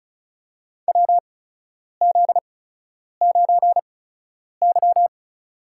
Morse code, heard as a keyed tone, about 36 wpm, spaced as for 10 wpm, 700 Hz. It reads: WZ9Y
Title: USA Call Signs - 4 Characters - 8x Character Spacing - 35wpm